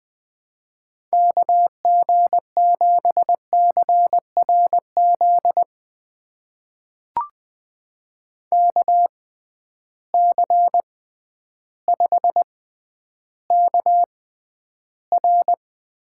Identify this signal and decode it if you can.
Morse code, heard as a keyed tone, about 20 wpm, spaced as for 11 wpm, 700 Hz. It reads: KG7CRZ KC5KR
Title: USA Call Signs - 6 Characters - 6x Character Spacing - 20wpm